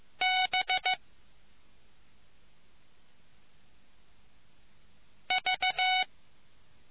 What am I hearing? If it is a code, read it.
Morse code, heard as a keyed tone, about 15 wpm, 740 Hz. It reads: B V